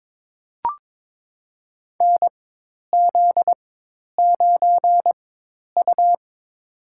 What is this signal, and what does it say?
Morse code, heard as a keyed tone, about 22 wpm, spaced as for 10 wpm, 700 Hz. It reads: NZ9U
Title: USA Call Signs - 4 Characters - 4x Character Spacing - 22wpm